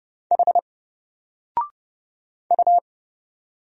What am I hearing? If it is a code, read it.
Morse code, heard as a keyed tone, about 30 wpm, 700 Hz. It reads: H U
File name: Single-Letter-Number-30wpm